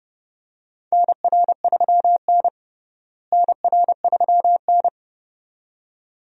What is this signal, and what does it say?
Morse code, heard as a keyed tone, about 30 wpm, 700 Hz. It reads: NR3N NR3N
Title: USA Call Signs - 4 Characters - Repeated 2x - 30wpm